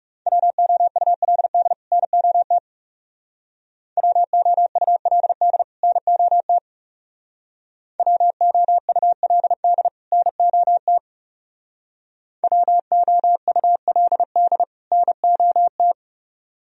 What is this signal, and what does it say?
Morse code, heard as a keyed tone, about 36 wpm, 700 Hz. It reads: WOULD NOT WOULD NOT WOULD NOT WOULD NOT
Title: Speed-Racing - Sets of 2 Words 30wpm